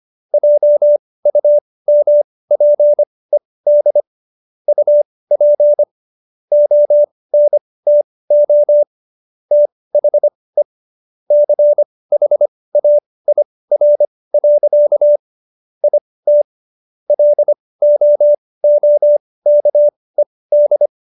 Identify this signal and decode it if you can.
Morse code, heard as a keyed tone, about 25 wpm, spaced as for 18 wpm, 590 Hz. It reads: JUMPED UP ONTO THE CHAIR. IT LOOKED